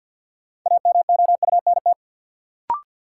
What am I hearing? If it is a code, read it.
Morse code, heard as a keyed tone, about 47 wpm, 700 Hz. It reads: AMOUNT
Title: Sets of 3 Words - New Word Drill - 50wpm